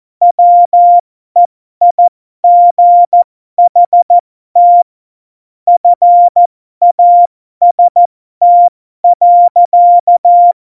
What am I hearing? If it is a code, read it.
Morse code, heard as a keyed tone, about 14 wpm, spaced as for 12 wpm, 700 Hz. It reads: WEIGHT FAST.
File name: Morsekurs / Witze